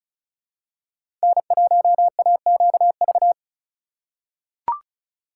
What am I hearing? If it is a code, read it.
Morse code, heard as a keyed tone, about 34 wpm, 700 Hz. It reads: N1AQV
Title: USA Call Signs - 5 Characters - 3x Character Spacing - 35wpm